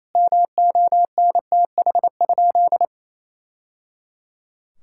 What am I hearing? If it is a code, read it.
Morse code, heard as a keyed tone, about 28 wpm, 700 Hz. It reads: MONTH?
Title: Sentences from Top 1000 Words - Review Part 5 - 28wpm